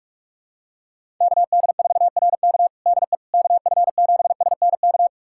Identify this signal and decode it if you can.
Morse code, heard as a keyed tone, about 41 wpm, 700 Hz. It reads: KD4RK DE KF7SNK
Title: USA Callsign DE Callsign 45wpm